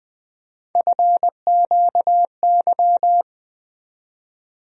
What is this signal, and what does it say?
Morse code, heard as a keyed tone, about 20 wpm, 700 Hz. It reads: FQY
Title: USA Call Sign Postfixes - 3 Letters 20wpm